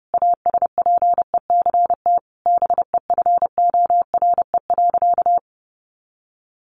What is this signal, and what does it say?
Morse code, heard as a keyed tone, about 30 wpm, 700 Hz. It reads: ASPECT BEFORE.
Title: Sentences from Top 1300 Words 30wpm